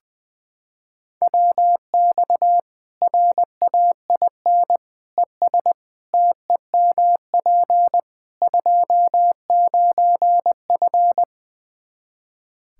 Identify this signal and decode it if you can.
Morse code, heard as a keyed tone, about 21 wpm, 700 Hz. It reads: WX RAIN ES TEMP 29F
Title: Practice QSO - Exchange 2 - Info - 20wpm